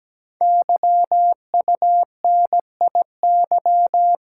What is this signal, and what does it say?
Morse code, heard as a keyed tone, about 17 wpm, 700 Hz. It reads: YUNIY
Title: Code Groups - 5 Letters - 17wpm